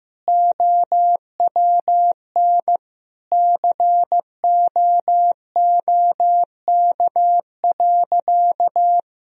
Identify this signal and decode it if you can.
Morse code, heard as a keyed tone, about 15 wpm, 700 Hz. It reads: OWN COOK.
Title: Sentences from Top 1000 Words - Review Part 4 - 15wpm